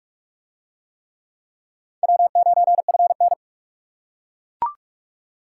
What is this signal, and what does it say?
Morse code, heard as a keyed tone, about 40 wpm, 700 Hz. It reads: W9FN